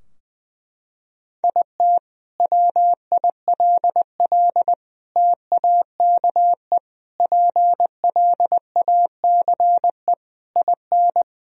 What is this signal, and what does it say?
Morse code, heard as a keyed tone, about 20 wpm, 700 Hz. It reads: IT WILL TAKE PLACE IN
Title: Sentences from Top 1000 Words - Review Part 2 - 20wpm